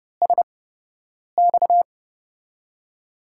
Morse code, heard as a keyed tone, about 30 wpm, spaced as for 8 wpm, 700 Hz. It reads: SX
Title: USA Call Signs - 8x Character Spacing - 30wpm